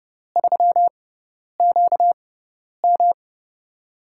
Morse code, heard as a keyed tone, about 32 wpm, spaced as for 10 wpm, 700 Hz. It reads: 3QM
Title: USA Call Signs - 6x Character Spacing - 30wpm